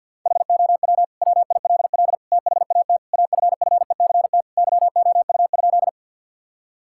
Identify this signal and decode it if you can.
Morse code, heard as a keyed tone, about 50 wpm, 700 Hz. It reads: HOW WILL THAT AFFECT YOU?
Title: Sentences from Top 1000 Words - Review Part 6 - 50wpm